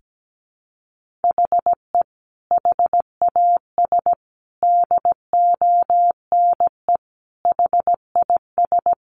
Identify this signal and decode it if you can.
Morse code, heard as a keyed tone, about 17 wpm, 700 Hz. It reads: HE HAS DONE HIS
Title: Sentences from Top 1500 Words 17wpm